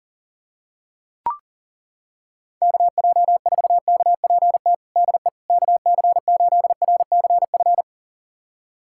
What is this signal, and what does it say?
Morse code, heard as a keyed tone, about 37 wpm, 700 Hz. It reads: KJ4KPT DE KC8RCF
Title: USA Callsign DE Callsign - 6 Characters - 40wpm